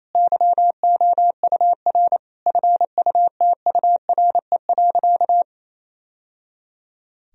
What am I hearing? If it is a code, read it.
Morse code, heard as a keyed tone, about 27 wpm, 700 Hz. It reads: YOUR FUTURE.
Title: Sentences from Top 1000 Words - Review Part 4 - 28wpm